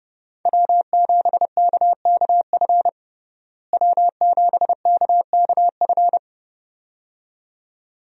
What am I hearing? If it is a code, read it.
Morse code, heard as a keyed tone, about 30 wpm, 700 Hz. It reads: W7KKF W7KKF
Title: USA Call Signs - 5 Characters - Repeated 2x - 30wpm